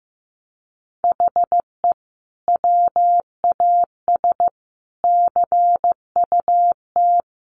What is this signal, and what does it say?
Morse code, heard as a keyed tone, about 15 wpm, 700 Hz. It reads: HE WAS CUT
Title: Sentences from Top 1100 Words 15wpm